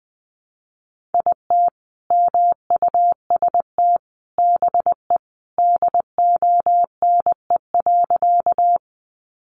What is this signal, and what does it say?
Morse code, heard as a keyed tone, about 20 wpm, 700 Hz. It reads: IT MUST BE DONE.
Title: Sentences from Top 500 Words 20wpm